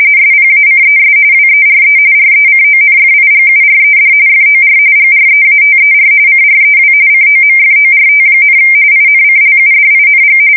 RTTY, about 45.45 baud. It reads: SPACES WERE RECONFIGURED FOR GREATER CREW COMFORT, INCLUDING
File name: fsk170 / battleshipnj